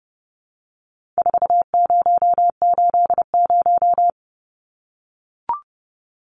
Morse code, heard as a keyed tone, about 30 wpm, 700 Hz. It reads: 4080
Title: Numbers - 4-Digit 30wpm